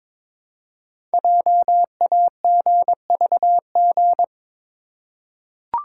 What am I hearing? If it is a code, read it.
Morse code, heard as a keyed tone, about 22 wpm, 700 Hz. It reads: JAGVG
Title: Code Groups - 5 Letters - 22wpm